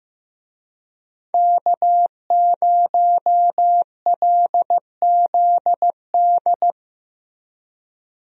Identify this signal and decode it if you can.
Morse code, heard as a keyed tone, about 15 wpm, 700 Hz. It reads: K0LZD
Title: USA Call Signs - 5 Characters - Repeated 3x - 15wpm